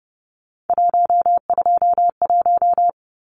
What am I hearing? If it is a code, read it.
Morse code, heard as a keyed tone, about 30 wpm, 700 Hz. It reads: 121
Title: Numbers - 3-Digit 30wpm